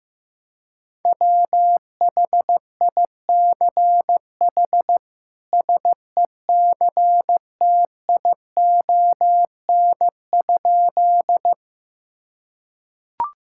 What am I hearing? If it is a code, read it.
Morse code, heard as a keyed tone, about 15 wpm, 700 Hz. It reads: WHICH SECTION?
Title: Sentences from Top 1000 Words - Review Part 5 - 15wpm